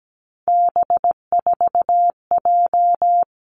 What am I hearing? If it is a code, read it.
Morse code, heard as a keyed tone, about 17 wpm, 700 Hz. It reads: B4J
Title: Three Challenging Letter-Number Combinations 17wpm